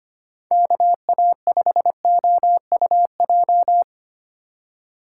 Morse code, heard as a keyed tone, about 25 wpm, 700 Hz. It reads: KA5OUJ
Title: USA Call Signs - 6 Characters - 3x Character Spacing - 25wpm